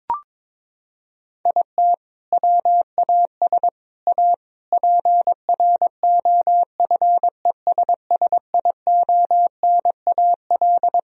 Morse code, heard as a keyed tone, about 21 wpm, 700 Hz. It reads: IT WAS A PROFESSIONAL
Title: Sentences from Top 1000 Words - Review Part 4 - 22wpm